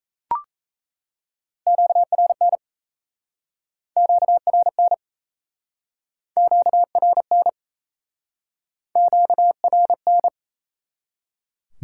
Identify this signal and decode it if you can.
Morse code, heard as a keyed tone, about 37 wpm, 700 Hz. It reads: QRN QRN QRN QRN